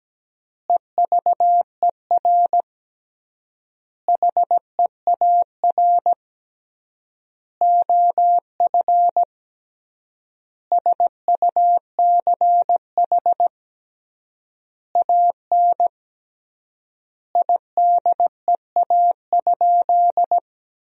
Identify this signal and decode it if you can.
Morse code, heard as a keyed tone, about 17 wpm, 700 Hz. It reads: EVER HEAR OF SUCH AN IDEA?
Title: 3x Word-Spacing - Sentences from Top 400 Words - 17wpm